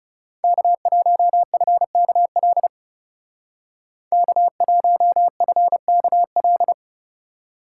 Morse code, heard as a keyed tone, about 37 wpm, 700 Hz. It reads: K1FKL K1FKL
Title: Speed Racing - USA 1x3 Call Signs 30wpm